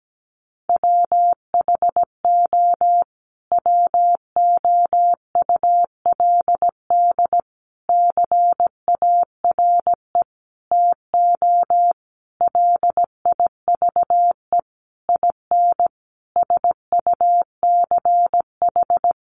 Morse code, heard as a keyed tone, about 17 wpm, 700 Hz. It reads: WHO WOULD CARE TO LIVE IN SUCH
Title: Sentences from Top 1500 Words 17wpm